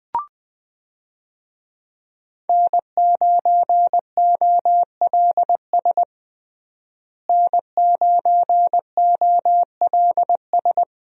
Morse code, heard as a keyed tone, about 21 wpm, 700 Hz. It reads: N9OLS N9OLS
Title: USA Call Signs - 5 Characters - Repeated 2x - 20wpm